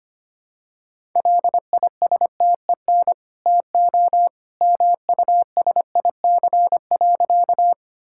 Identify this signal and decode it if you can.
Morse code, heard as a keyed tone, about 25 wpm, 690 Hz. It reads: LISTEN TO MUSIC.
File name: Easy Sentences with Hints - 25wpm